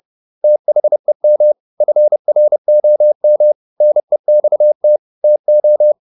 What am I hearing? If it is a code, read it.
Morse code, heard as a keyed tone, about 30 wpm, 600 Hz. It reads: THEM FROM NEXT TO
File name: the old man and the sea / the old man and the sea 1 30wpm